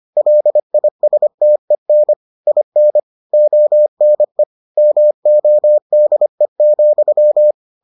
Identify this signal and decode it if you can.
Morse code, heard as a keyed tone, about 25 wpm, 600 Hz. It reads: LISTEN IN ONE MODE,